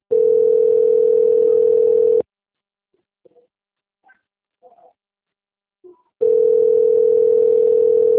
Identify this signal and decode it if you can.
Ringback tone (North America), 440 + 480 Hz.
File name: out- - - - - .38975